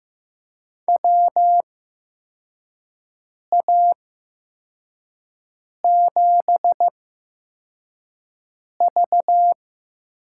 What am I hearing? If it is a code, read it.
Morse code, heard as a keyed tone, about 15 wpm, spaced as for 4 wpm, 700 Hz. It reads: WA7V
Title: USA Call Signs - 8x Character Spacing - 15wpm